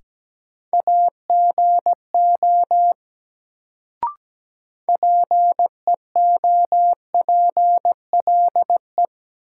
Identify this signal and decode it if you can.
Morse code, heard as a keyed tone, about 17 wpm, 700 Hz. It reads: AGO PEOPLE
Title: Sets of 2 Words - New Word Drill - 17wpm